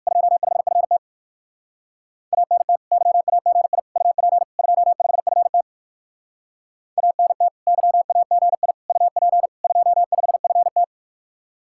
Morse code, heard as a keyed tone, about 69 wpm, spaced as for 59 wpm, 700 Hz. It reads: 25FT ANT YAGI UP 25FT ANT YAGI UP 25FT